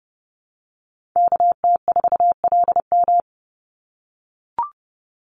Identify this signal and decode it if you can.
Morse code, heard as a keyed tone, about 30 wpm, 700 Hz. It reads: KT4LM
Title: USA Call Signs 30wpm